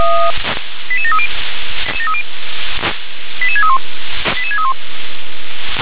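Five-tone selective call (ZVEI), heard as quick runs of five tones, tones of about 70 ms. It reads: 88730 88631 88631
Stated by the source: ZVEI five-tone selective call received by SDR.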